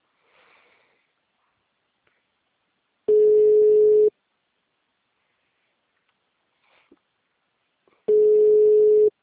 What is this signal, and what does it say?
Ringback tone (Europe, 425 Hz).
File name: out- -2001- - - .54